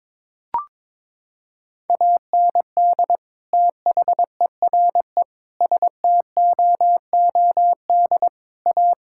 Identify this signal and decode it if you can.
Morse code, heard as a keyed tone, about 22 wpm, 700 Hz. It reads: AND THERE STOOD A
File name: practice / 22wpm / Sentences with 2-Word Phrases - 22wpm